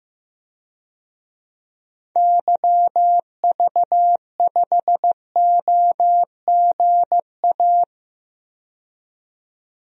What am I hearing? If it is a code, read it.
Morse code, heard as a keyed tone, about 15 wpm, 700 Hz. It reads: YV5OGA
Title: DXCC-100 Call Signs - 6 Characters - 15wpm